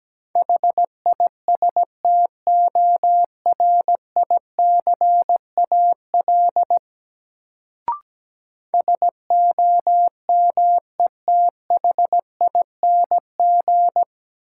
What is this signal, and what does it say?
Morse code, heard as a keyed tone, about 17 wpm, 700 Hz. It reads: HISTORICAL SOMETHING
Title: Sentences from Top 600 Words - New Word Drill - 17wpm